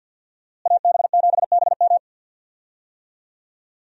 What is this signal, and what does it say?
Morse code, heard as a keyed tone, about 50 wpm, 700 Hz. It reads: AB7BM
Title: USA Call Signs - 5 Characters - Repeated 3x - 50wpm